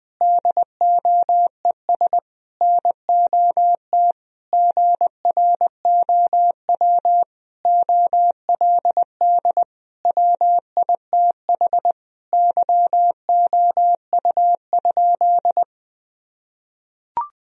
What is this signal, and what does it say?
Morse code, heard as a keyed tone, about 20 wpm, 700 Hz. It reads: DOES NOT GROW OLD WITH YOU?
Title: Sentences from Top 1000 Words - Review Part 1 - 20wpm